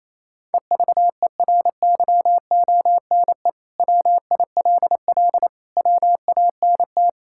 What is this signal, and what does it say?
Morse code, heard as a keyed tone, about 28 wpm, 700 Hz. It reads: EVERYONE WILL WANT
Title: Sentences from Top 1000 Words - Review Part 5 - 28wpm